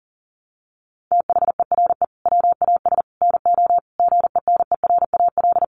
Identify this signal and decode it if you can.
Morse code, heard as a keyed tone, about 40 wpm, 700 Hz. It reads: THERE WAS NO GENERAL
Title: Sentences from Top 1200 Words 40wpm